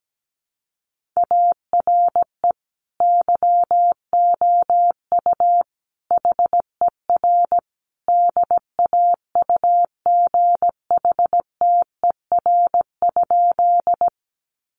Morse code, heard as a keyed tone, about 17 wpm, 700 Hz. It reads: ARE YOU HER DAUGHTER?
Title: Sentences from Top 1500 Words 17wpm